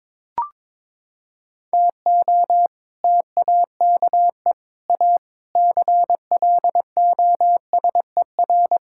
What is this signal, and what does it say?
Morse code, heard as a keyed tone, about 22 wpm, 700 Hz. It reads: TO TAKE A CLOSER
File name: practice / 22wpm / Sets of 7 Words - 22wpm